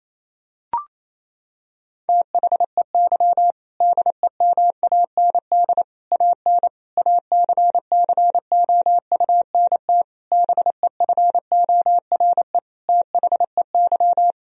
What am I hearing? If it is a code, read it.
Morse code, heard as a keyed tone, about 28 wpm, 700 Hz. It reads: THEY DEMAND AN ACCOUNT BEFORE THEY
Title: Sentences from Top 1000 Words - Review Part 4 - 28wpm